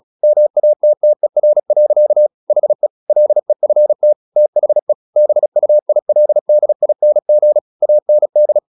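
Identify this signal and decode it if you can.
Morse code, heard as a keyed tone, about 36 wpm, 600 Hz. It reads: MATTER. HE LEFT THE BUILDING AND
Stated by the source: Morse practice at 36 wpm, text of Ken Follett's novel Never, from CW med Gnister.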